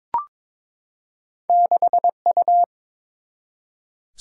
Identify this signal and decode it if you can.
Morse code, heard as a keyed tone, about 23 wpm, 700 Hz. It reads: 6U
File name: practice / 22wpm / Two Letter-Number Combinations - 22wpm